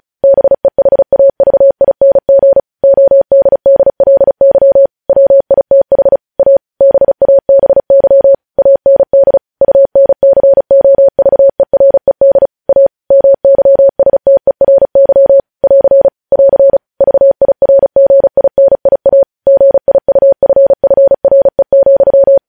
Morse code, heard as a keyed tone, about 36 wpm, 550 Hz. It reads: BEHAVING ODDLY WITH A BABY AND UNCOVERED A MYSTERY <AR> <AR> VIRGINIA GIUFFRE,
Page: News headlines in Morse code at 35 wpm